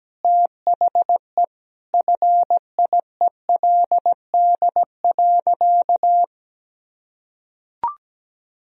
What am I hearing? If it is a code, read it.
Morse code, heard as a keyed tone, about 17 wpm, 700 Hz. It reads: THE FIELD.
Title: Sentences from Top 1000 Words - Review Part 2 - 17wpm